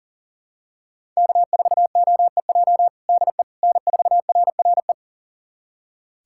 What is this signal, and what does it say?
Morse code, heard as a keyed tone, about 40 wpm, 700 Hz. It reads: K4OEJ DE N4RRE
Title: USA Callsign DE Callsign - 5 Characters - 40wpm